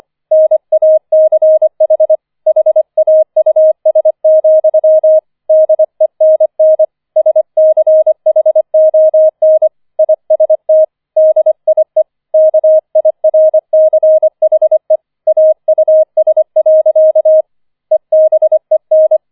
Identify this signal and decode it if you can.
Morse code, heard as a keyed tone, about 25 wpm, 630 Hz. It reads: NACH HAUS, DENN SCHON IST DIE KIRCHE AUS. EBEN
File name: 004 Vierter Streich 25wpm